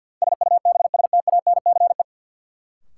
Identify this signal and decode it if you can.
Morse code, heard as a keyed tone, about 52 wpm, 700 Hz. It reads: SUBSTANCE
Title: Sentences from Top 1500 Words - New Word Drill Alt - 50wpm